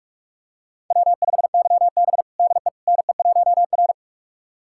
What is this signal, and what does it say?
Morse code, heard as a keyed tone, about 42 wpm, 700 Hz. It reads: W5YB DE NE1R
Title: USA Callsign DE Callsign - 4 Characters - 45wpm